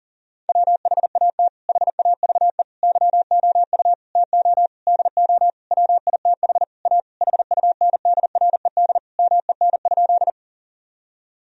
Morse code, heard as a keyed tone, about 40 wpm, 700 Hz. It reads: WHAT HAVE YOU TO DO WITH A HUNDRED MEN?
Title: Sentences from Top 1000 Words - Review Part 2 - 40wpm